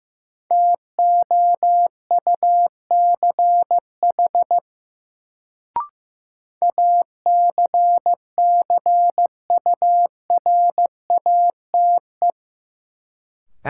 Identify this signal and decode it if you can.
Morse code, heard as a keyed tone, about 15 wpm, 700 Hz. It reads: TOUCH ACCURATE
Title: Sentences from Top 1200 Words - New Word Drill - 15wpm